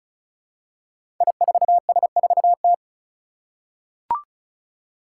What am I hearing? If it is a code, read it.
Morse code, heard as a keyed tone, about 36 wpm, 700 Hz. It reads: I4S4T
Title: Five Challenging Letter-Number Combinations 35wpm